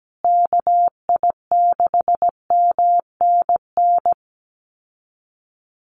Morse code, heard as a keyed tone, about 17 wpm, 700 Hz. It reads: KI6MNN
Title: USA Call Signs Repeated 2x 17wpm